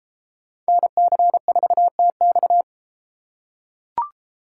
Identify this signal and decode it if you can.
Morse code, heard as a keyed tone, about 31 wpm, 700 Hz. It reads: NC4TX